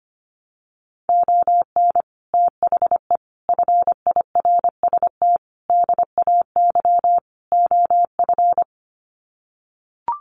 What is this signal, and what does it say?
Morse code, heard as a keyed tone, about 25 wpm, 700 Hz. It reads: ON THE FIRST DAY OF
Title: Sets of 5 Words Encore 25wpm